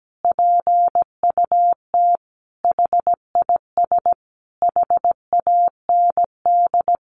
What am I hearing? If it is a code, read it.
Morse code, heard as a keyed tone, about 17 wpm, 700 Hz. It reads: PUT HIS HAND
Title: A Sound Of Thunder 17wpm